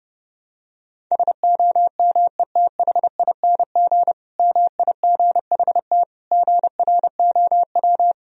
Morse code, heard as a keyed tone, about 32 wpm, 700 Hz. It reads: SOMETHING MIGHT GROW
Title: Sentences from Top 1000 Words - Review Part 1 - 30wpm